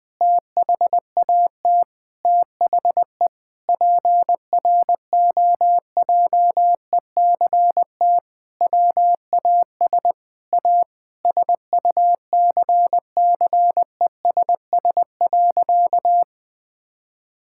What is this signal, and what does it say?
Morse code, heard as a keyed tone, about 21 wpm, 700 Hz. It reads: THAT THE PROJECT WAS A SUCCESS.
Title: Sentences with 8-Word Phrases - Spoken Beginning - 20wpm